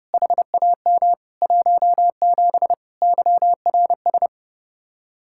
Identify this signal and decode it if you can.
Morse code, heard as a keyed tone, about 37 wpm, 700 Hz. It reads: HAM 17 YRS
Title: Speed-Racing Common QSO Phrases Encore - 30wpm